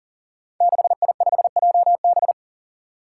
Morse code, heard as a keyed tone, about 43 wpm, 700 Hz. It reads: 6I5JB
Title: Five Challenging Letter-Number Combinations 40wpm